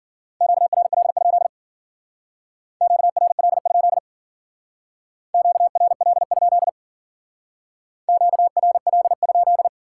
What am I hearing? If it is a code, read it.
Morse code, heard as a keyed tone, about 52 wpm, 700 Hz. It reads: QRL? QRL? QRL? QRL?